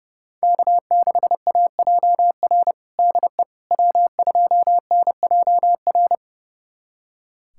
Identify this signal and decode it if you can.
Morse code, heard as a keyed tone, about 30 wpm, 700 Hz. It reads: K6AJR DE W2NJR
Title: USA Callsign DE Callsign - 5 Characters - 30wpm